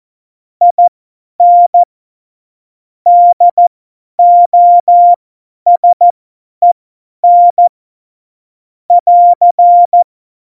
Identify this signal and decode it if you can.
Morse code, heard as a keyed tone, about 14 wpm, spaced as for 10 wpm, 700 Hz. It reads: IN DOSEN <AR>
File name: Morsekurs / Witze / Witz 03.Juli 2018, 50 F70